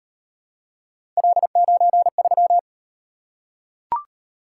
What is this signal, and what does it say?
Morse code, heard as a keyed tone, about 36 wpm, 700 Hz. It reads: L93